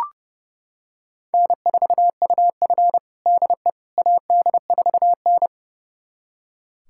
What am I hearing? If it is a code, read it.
Morse code, heard as a keyed tone, about 31 wpm, 700 Hz. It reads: N4UF DE AD4N
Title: USA Callsign DE Callsign - 4 Characters - 30wpm